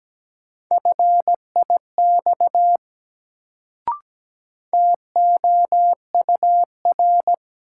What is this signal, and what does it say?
Morse code, heard as a keyed tone, about 17 wpm, 700 Hz. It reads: FIX TOUR
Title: Sentences from Top 1400 Words - New Word Drill Alt - 17wpm